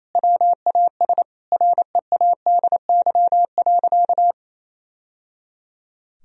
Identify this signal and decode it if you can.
Morse code, heard as a keyed tone, about 28 wpm, 690 Hz. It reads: WAS READY.
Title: Sentences from Top 1000 Words - Review Part 3 - 28wpm